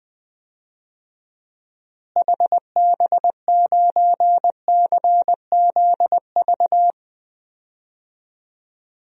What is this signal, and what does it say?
Morse code, heard as a keyed tone, about 21 wpm, 700 Hz. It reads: HB9CZV
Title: DXCC-100 Call Signs - 6 Characters - Repeated 3x - 20wpm